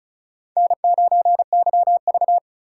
Morse code, heard as a keyed tone, about 36 wpm, 700 Hz. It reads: N9YV
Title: USA Call Signs - 4 Characters - 3x Character Spacing - 35wpm